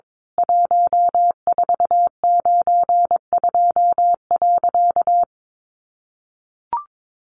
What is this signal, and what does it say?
Morse code, heard as a keyed tone, about 22 wpm, 700 Hz. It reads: 1492.
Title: A Sound Of Thunder 22wpm